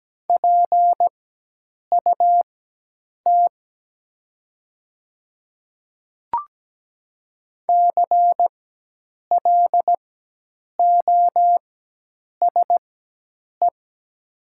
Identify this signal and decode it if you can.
Morse code, heard as a keyed tone, about 17 wpm, spaced as for 8 wpm, 700 Hz. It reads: PUT CLOSE
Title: 4x Character Spacing - Top 300 Words - 17wpm